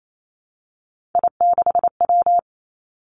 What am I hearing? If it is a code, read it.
Morse code, heard as a keyed tone, about 28 wpm, 700 Hz. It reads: I6W
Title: Three Challenging Letter-Number Combinations 28wpm